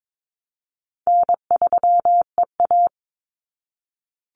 Morse code, heard as a keyed tone, about 22 wpm, 700 Hz. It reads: N3EA
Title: Easy Call Signs - 22wpm